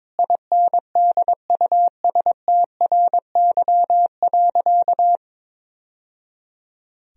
Morse code, heard as a keyed tone, about 22 wpm, 700 Hz. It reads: INDUSTRY.
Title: Sentences from Top 1000 Words - Review Part 4 - 22wpm